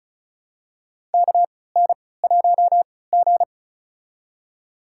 Morse code, heard as a keyed tone, about 36 wpm, spaced as for 20 wpm, 700 Hz. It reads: KN1G
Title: USA Call Signs - 4 Characters - 3x Character Spacing - 35wpm